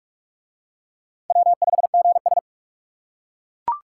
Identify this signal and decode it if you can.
Morse code, heard as a keyed tone, about 40 wpm, 700 Hz. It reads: W5GS